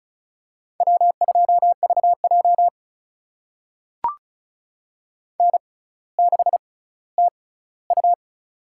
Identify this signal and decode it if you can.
Morse code, heard as a keyed tone, about 36 wpm, spaced as for 18 wpm, 700 Hz. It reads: W2VJ N6TU
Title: USA Call Signs - 4 Characters - 6x Character Spacing - 35wpm